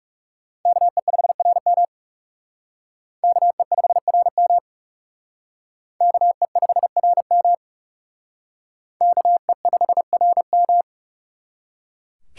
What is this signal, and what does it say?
Morse code, heard as a keyed tone, about 37 wpm, 700 Hz. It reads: KE5RM KE5RM KE5RM KE5RM
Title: Speed Racing - USA 2x2 Call Signs 30wpm